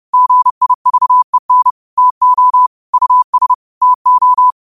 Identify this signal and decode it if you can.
Morse code, heard as a keyed tone, about 30 wpm, 1000 Hz. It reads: GIVEN TO US TO
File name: qotdfiles / 30wpm